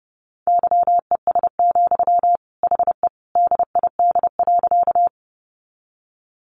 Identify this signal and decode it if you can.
Morse code, heard as a keyed tone, about 30 wpm, 700 Hz. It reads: YES, HE DID.
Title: Sentences from Top 500 Words 30wpm